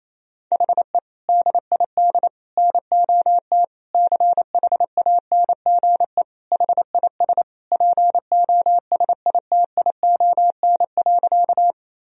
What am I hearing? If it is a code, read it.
Morse code, heard as a keyed tone, about 28 wpm, 700 Hz. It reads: HE DID NOT CHANGE HIS POSITION.
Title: Sentences from Top 1000 Words - Review Part 1 - 28wpm